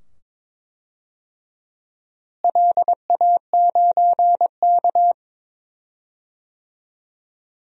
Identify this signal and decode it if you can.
Morse code, heard as a keyed tone, about 21 wpm, 700 Hz. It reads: LA9K
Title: DXCC-100 Call Signs - 4 Characters - Repeated 2x - 22wpm